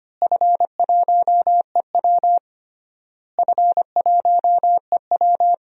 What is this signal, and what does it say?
Morse code, heard as a keyed tone, about 25 wpm, 700 Hz. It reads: F1EW F1EW
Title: DXCC-100 Call Signs - 4 Characters - Repeated 3x - 25wpm